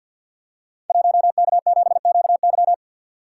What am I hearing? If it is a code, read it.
Morse code, heard as a keyed tone, about 48 wpm, 700 Hz. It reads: 1K7QY
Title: Code Groups - 5 Letter:Numbers - 50wpm